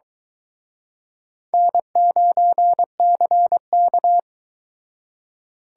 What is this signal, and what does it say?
Morse code, heard as a keyed tone, about 23 wpm, 700 Hz. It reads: N9CK